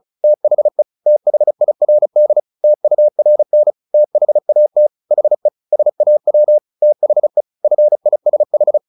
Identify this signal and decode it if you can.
Morse code, heard as a keyed tone, about 35 wpm, 600 Hz. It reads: THE THIRD TURN THAT HE SAW THE FISH
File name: the old man and the sea 2 35wpm